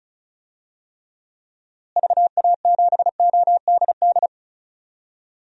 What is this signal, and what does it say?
Morse code, heard as a keyed tone, about 36 wpm, 700 Hz. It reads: VA7ODD
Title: DXCC-100 Call Signs - 6 Characters - Repeated 3x - 35wpm